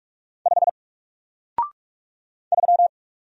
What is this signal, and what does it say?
Morse code, heard as a keyed tone, about 45 wpm, 700 Hz. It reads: 5 3
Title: Single Character (Letter:Number:Punctuation:Prosign) Alt 45wpm